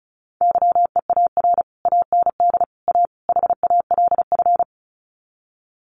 Morse code, heard as a keyed tone, about 34 wpm, 700 Hz. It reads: YEAR AND A HALF
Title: Sets of 4 Words 35wpm